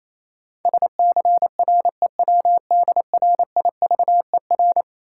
Morse code, heard as a keyed tone, about 28 wpm, 700 Hz. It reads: SCREWDRIVER
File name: Easy Words - Speed-Racing with Hints - 22wpm